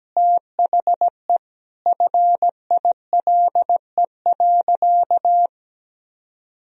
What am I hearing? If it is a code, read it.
Morse code, heard as a keyed tone, about 17 wpm, 700 Hz. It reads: THE FILE.
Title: Sentences from Top 1000 Words - Review Part 4 - 17wpm